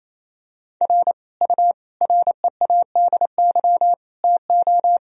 Morse code, heard as a keyed tone, about 28 wpm, 700 Hz. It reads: R U READY TO
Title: Sentences from Top 400 Words and Abbreviations 28wpm